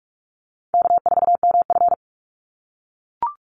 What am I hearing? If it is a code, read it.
Morse code, heard as a keyed tone, about 44 wpm, 700 Hz. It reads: K4MF
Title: NA SOTA Chaser Callsigns 45wpm